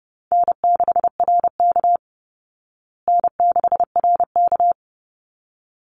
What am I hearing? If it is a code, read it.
Morse code, heard as a keyed tone, about 30 wpm, 700 Hz. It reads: N6RK N6RK
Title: Easy Call Signs Repeated 2x - 30wpm